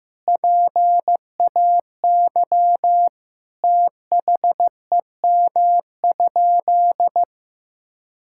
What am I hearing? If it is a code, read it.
Morse code, heard as a keyed tone, about 15 wpm, 700 Hz. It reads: PAY THEM?
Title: Sentences from Top 1000 Words - Review Part 5 - 15wpm